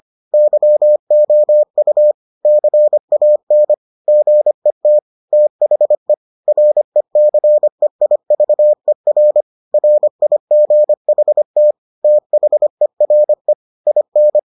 Morse code, heard as a keyed tone, about 25 wpm, 600 Hz. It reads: YOU CAN GET THE RECEIVER RIGHT THERE IN